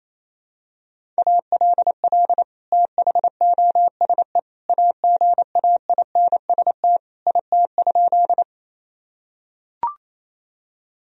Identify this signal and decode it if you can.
Morse code, heard as a keyed tone, about 28 wpm, 700 Hz. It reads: ALL THOSE AGAINST IT?
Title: Sentences from Top 1000 Words - Review Part 1 - 28wpm